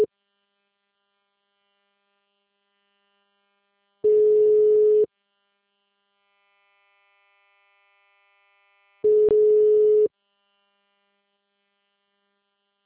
Ringback tone (Europe, 425 Hz).
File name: out- -2002- - - .36612